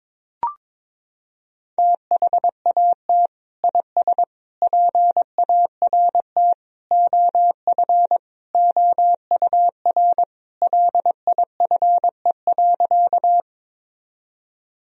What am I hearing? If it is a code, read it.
Morse code, heard as a keyed tone, about 21 wpm, 700 Hz. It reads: THAT IS PART OF OUR LIFE.
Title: Sentences from Top 1000 Words - Review Part 1 - 22wpm